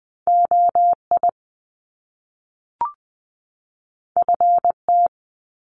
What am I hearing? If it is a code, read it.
Morse code, heard as a keyed tone, about 20 wpm, 700 Hz. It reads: O I F T
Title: Top Two Letter Combinations in Words 20wpm